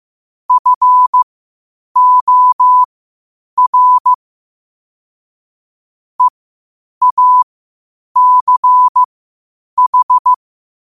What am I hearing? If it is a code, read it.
Morse code, heard as a keyed tone, about 15 wpm, spaced as for 8 wpm, 1000 Hz. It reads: FOR EACH
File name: qotdfiles / 8wpm